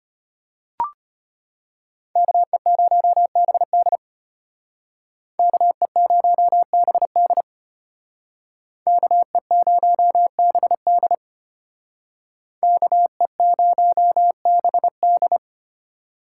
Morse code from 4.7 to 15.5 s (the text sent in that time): KE0BD KE0BD KE0BD